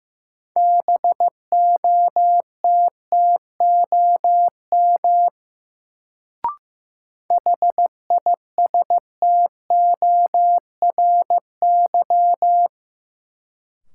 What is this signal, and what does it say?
Morse code, heard as a keyed tone, about 15 wpm, 700 Hz. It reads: BOTTOM HISTORY
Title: Sets of 3 Words - New Word Drill - 15wpm